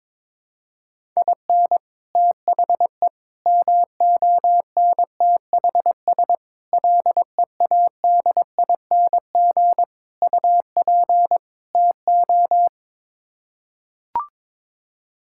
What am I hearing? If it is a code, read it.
Morse code, heard as a keyed tone, about 22 wpm, 700 Hz. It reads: IN THE MONTHS LEADING UP TO
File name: practice / 22wpm / Sets of 6 Words - 22wpm